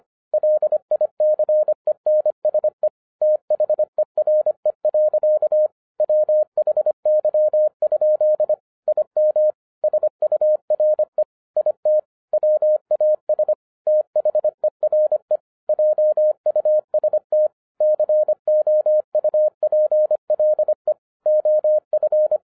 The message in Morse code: LICENSE THERE. WHY? IM SURE IT WAS THERE JUST COUPLE OF

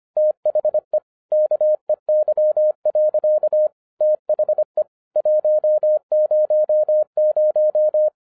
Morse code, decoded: THE KEY. THE 100

